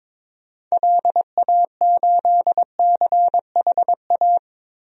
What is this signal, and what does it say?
Morse code, heard as a keyed tone, about 22 wpm, 700 Hz. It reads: LA8CHA
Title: DXCC-100 Call Signs - 6 Characters - 22wpm